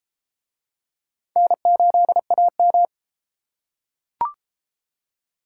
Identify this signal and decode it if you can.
Morse code, heard as a keyed tone, about 30 wpm, 700 Hz. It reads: N8AM